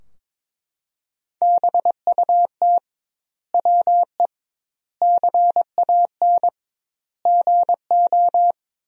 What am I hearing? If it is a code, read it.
Morse code, heard as a keyed tone, about 22 wpm, 700 Hz. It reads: BUT WE CAN GO
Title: 2x Word-Spacing - Sentences from Top 100 Words - 22wpm